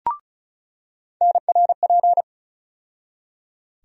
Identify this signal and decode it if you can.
Morse code, heard as a keyed tone, about 36 wpm, 700 Hz. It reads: NRP
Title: USA Call Sign Postfixes - 3 Letters 35wpm